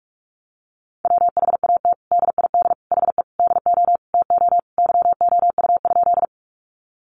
Morse code, heard as a keyed tone, about 44 wpm, 700 Hz. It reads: WHAT DID HE DO TO YOU?
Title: Sentences from Top 100 Words 45wpm